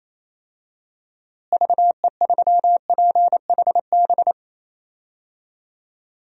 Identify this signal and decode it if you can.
Morse code, heard as a keyed tone, about 28 wpm, 700 Hz. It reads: VE3PHB
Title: DXCC-100 Call Signs - 6 Characters - Repeated 3x - 28wpm